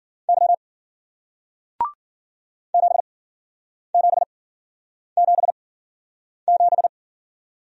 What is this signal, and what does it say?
Morse code, heard as a keyed tone, about 53 wpm, spaced as for 9 wpm, 700 Hz. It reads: Y 7777